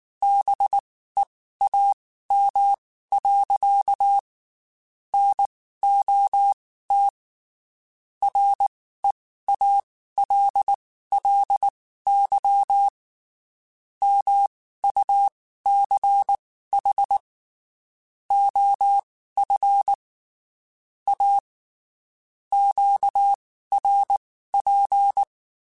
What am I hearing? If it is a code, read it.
Morse code, heard as a keyed tone, about 19 wpm, spaced as for 14 wpm, 800 Hz. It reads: BEAM. NOT REALLY MUCH OF A QRP
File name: QSOs / QSOs-mp3s / WD0BGP 1@18